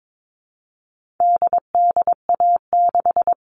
Morse code, heard as a keyed tone, about 22 wpm, 700 Hz. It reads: DDA6
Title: All Cut Numbers - 4-Digit 22wpm